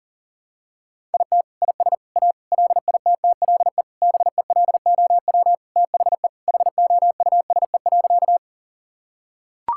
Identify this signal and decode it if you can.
Morse code, heard as a keyed tone, about 37 wpm, 700 Hz. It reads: IT IS A LITTLE BELOW THE HOUSE.
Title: Sentences from Top 1000 Words - Review Part 5 - 40wpm